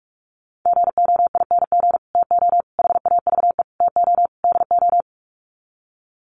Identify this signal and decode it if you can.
Morse code, heard as a keyed tone, about 44 wpm, 700 Hz. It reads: GOING TO HAVE TO DO
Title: Sets of 5 Words Encore 45wpm